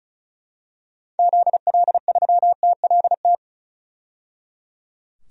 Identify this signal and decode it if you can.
Morse code, heard as a keyed tone, about 36 wpm, 700 Hz. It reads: ZL3TLT
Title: DXCC-100 Call Signs - 6 Characters - 35wpm